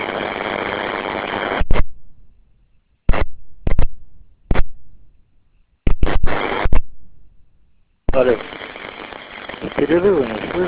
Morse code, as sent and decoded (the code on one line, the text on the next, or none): none